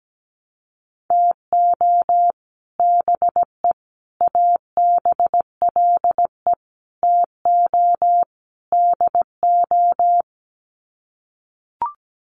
Morse code, heard as a keyed tone, about 17 wpm, 700 Hz. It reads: TO BE ABLE TO DO
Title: Sets of 5 Words 17wpm